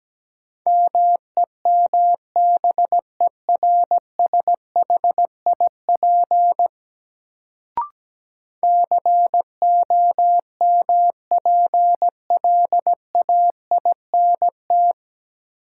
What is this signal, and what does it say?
Morse code, heard as a keyed tone, about 17 wpm, 700 Hz. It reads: MEMBERSHIP COMPLAINT
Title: Sentences from Top 1400 Words - New Word Drill Alt - 17wpm